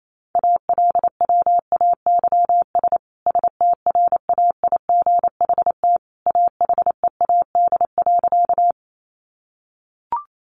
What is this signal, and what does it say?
Morse code, heard as a keyed tone, about 28 wpm, 700 Hz. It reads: ALWAYS STRAIGHT AHEAD.
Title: Sentences from Top 1100 Words 28wpm